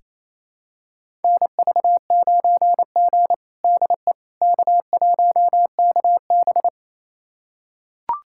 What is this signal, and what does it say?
Morse code, heard as a keyed tone, about 28 wpm, 700 Hz. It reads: NV9G DE K1KB
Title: USA Callsign DE Callsign - 4 Characters - 28wpm